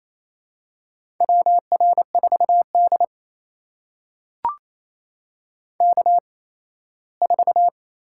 Morse code, heard as a keyed tone, about 27 wpm, spaced as for 17 wpm, 700 Hz. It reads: WR4D K4
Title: USA Call Signs - 4 Characters - 8x Character Spacing - 28wpm